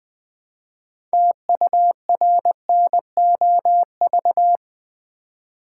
Morse code, heard as a keyed tone, about 20 wpm, 700 Hz. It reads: TURNOV
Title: Build A Word - 3 Letters At A Time - 8-Letter Words - 20wpm